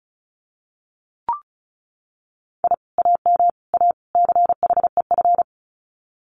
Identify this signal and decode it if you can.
Morse code, heard as a keyed tone, about 34 wpm, 700 Hz. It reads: I AM A CHEF
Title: Common QSO Phrases Encore 35wpm